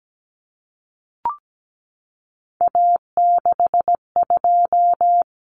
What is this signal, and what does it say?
Morse code, heard as a keyed tone, about 17 wpm, 690 Hz. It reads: A62
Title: Common Cut Numbers - 3-Digit 17wpm